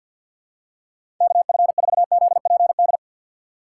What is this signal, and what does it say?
Morse code, heard as a keyed tone, about 48 wpm, 700 Hz. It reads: KF4ZPD
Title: USA Call Signs - 6 Characters - 6x Character Spacing - 50wpm